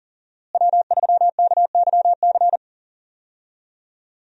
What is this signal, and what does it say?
Morse code, heard as a keyed tone, about 40 wpm, 700 Hz. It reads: W3KYC
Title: USA Call Signs - 5 Characters - Repeated 2x - 40wpm